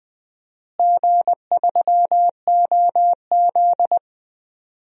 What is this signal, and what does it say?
Morse code, heard as a keyed tone, about 21 wpm, 700 Hz. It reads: G3OZ
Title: DXCC-100 Call Signs - 4 Characters - Repeated 2x - 20wpm